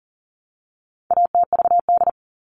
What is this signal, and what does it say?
Morse code, heard as a keyed tone, about 40 wpm, 700 Hz. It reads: ATVD
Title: All Cut Numbers - 4-Digit 40wpm